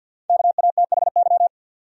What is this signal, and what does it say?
Morse code, heard as a keyed tone, about 52 wpm, 700 Hz. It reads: KATHY
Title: USA Common Names - Max 5 Letters - 50wpm